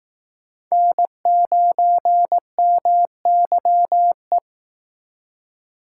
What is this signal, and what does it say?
Morse code, heard as a keyed tone, about 19 wpm, 700 Hz. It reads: N9MYE
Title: Speed Racing - USA 1x3 Call Signs 15wpm